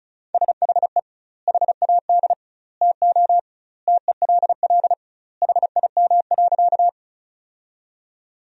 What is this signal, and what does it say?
Morse code, heard as a keyed tone, about 34 wpm, 700 Hz. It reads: SHE HAD TO TELL HIM.
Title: 2x Word-Spacing - Sentences from Top 200 Words - 35wpm